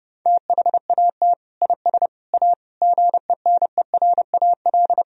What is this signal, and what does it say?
Morse code, heard as a keyed tone, about 30 wpm, 700 Hz. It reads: THAT IS A GENERAL
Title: Sentences from Top 1000 Words - Review Part 3 - 30wpm